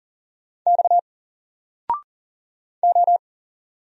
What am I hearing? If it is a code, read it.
Morse code, heard as a keyed tone, about 40 wpm, 700 Hz. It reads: X O
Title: Single Letter-Number Alt 40wpm